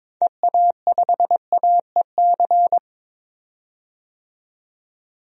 Morse code, heard as a keyed tone, about 22 wpm, 700 Hz. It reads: EA5AEC
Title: DXCC-100 Call Signs - 6 Characters - 22wpm